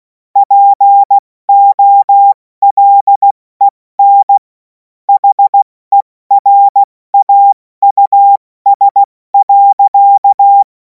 Morse code, heard as a keyed tone, about 16 wpm, spaced as for 14 wpm, 800 Hz. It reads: POLEN HERAUS.
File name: Morsekurs / Witze / Witz 03. März 2026 70 F80